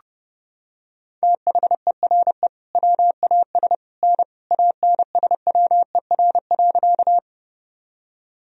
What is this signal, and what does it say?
Morse code, heard as a keyed tone, about 28 wpm, 700 Hz. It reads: THERE WAS N ANSWER.
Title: Sentences from Top 300 Words and Abbreviations 30wpm